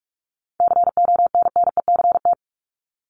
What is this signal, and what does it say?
Morse code, heard as a keyed tone, about 44 wpm, 700 Hz. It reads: CONNECT
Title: Top 1251-1500 Words 45wpm